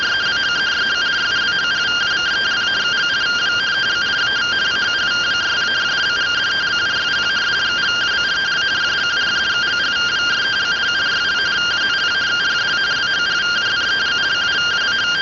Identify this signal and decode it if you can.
Two-tone teleprinter signal, about 170 Hz shift.